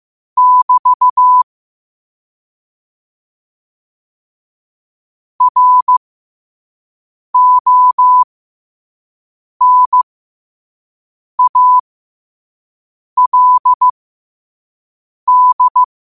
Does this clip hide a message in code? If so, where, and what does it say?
Yes: Morse code from 0.4 to 15.8 s, heard as a keyed tone, about 15 wpm, spaced as for 5 wpm, 1000 Hz. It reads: <BT> RONALD